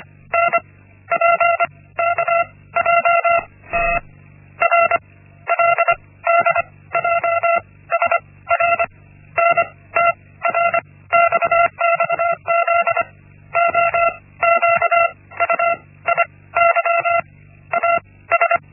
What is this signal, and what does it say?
Morse code, heard as a keyed tone, about 25 wpm, spaced as for 15 wpm, 680 Hz. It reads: NPKJT RLDJS RNTRX XZOQU IYAS
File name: Linksmakalnis-1